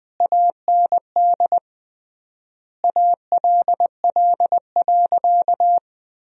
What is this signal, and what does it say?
Morse code, heard as a keyed tone, about 20 wpm, 700 Hz. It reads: AND ALL.
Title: 3x Word-Spacing - Sentences from Top 200 Words - 20wpm